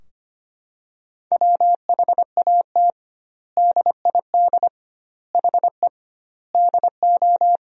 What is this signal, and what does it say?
Morse code, heard as a keyed tone, about 25 wpm, 700 Hz. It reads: WHAT DID HE DO
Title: 2x Word-Spacing - Sentences from Top 100 Words - 25wpm